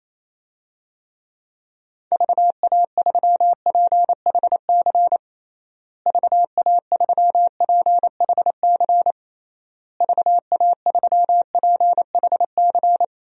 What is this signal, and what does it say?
Morse code, heard as a keyed tone, about 28 wpm, 700 Hz. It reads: VA3PHC VA3PHC VA3PHC
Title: DXCC-100 Call Signs - 6 Characters - Repeated 3x - 28wpm